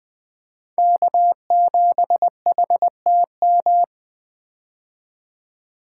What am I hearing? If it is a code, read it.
Morse code, heard as a keyed tone, about 20 wpm, 700 Hz. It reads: K7HTM
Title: 2023 POTA Activator Call Signs - 20wpm